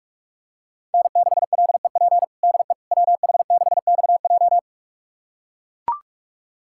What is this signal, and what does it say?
Morse code, heard as a keyed tone, about 41 wpm, 700 Hz. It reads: N6LEP DE WH6XJ
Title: USA Callsign DE Callsign - 5 Characters - 45wpm